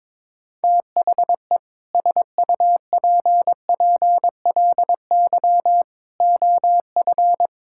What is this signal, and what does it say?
Morse code, heard as a keyed tone, about 21 wpm, 700 Hz. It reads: THE SUPPLY OF
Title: Sentences from Top 1000 Words - Review Part 6 - 22wpm